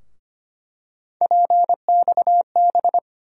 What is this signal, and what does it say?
Morse code, heard as a keyed tone, about 25 wpm, 700 Hz. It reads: PXB
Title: USA Call Sign Postfixes - 3 Letters 25wpm